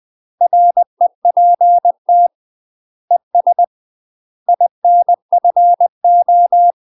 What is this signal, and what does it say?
Morse code, heard as a keyed tone, about 20 wpm, 700 Hz. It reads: REPT ES INFO
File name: UK MASTER QSOs TRAINING / 20WPM / UK MASTER QSOs 0003